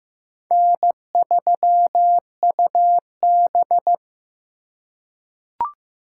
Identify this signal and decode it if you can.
Morse code, heard as a keyed tone, about 15 wpm, 700 Hz. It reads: N3UB
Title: USA Call Signs - 4 Characters - 2x Character Spacing - 15wpm